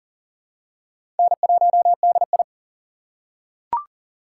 Morse code, heard as a keyed tone, about 39 wpm, 700 Hz. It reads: N1DI
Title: USA Call Signs - 4 Characters - 2x Character Spacing - 40wpm